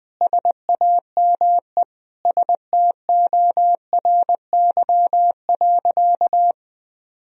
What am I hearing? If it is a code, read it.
Morse code, heard as a keyed tone, about 19 wpm, 700 Hz. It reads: SAME STORY.
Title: Sentences from Top 1000 Words - Review Part 1 - 20wpm